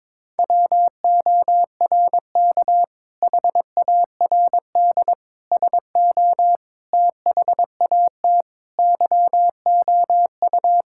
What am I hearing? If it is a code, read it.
Morse code, heard as a keyed tone, about 22 wpm, 700 Hz. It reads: WORK HARD SO THAT YOU
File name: practice / 22wpm / Sentences with 2-Word Phrases - 22wpm